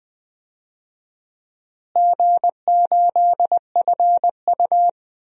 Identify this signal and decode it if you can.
Morse code, heard as a keyed tone, about 20 wpm, 700 Hz. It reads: G8FU
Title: DXCC-100 Call Signs - 4 Characters - Repeated 2x - 20wpm